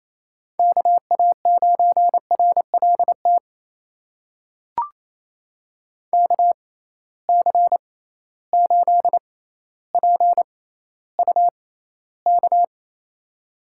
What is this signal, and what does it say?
Morse code, heard as a keyed tone, about 28 wpm, spaced as for 14 wpm, 700 Hz. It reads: KA9RLT KC8PUK
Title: USA Call Signs - 6 Characters - 6x Character Spacing - 28wpm